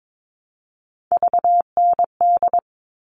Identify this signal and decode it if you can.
Morse code, heard as a keyed tone, about 22 wpm, 700 Hz. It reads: VND
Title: All Cut Numbers - 3-Digit 22wpm